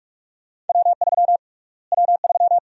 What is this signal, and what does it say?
Morse code, heard as a keyed tone, about 42 wpm, 700 Hz. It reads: W3 W3
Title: USA Call Sign Prefixes - Repeated 2x - 45wpm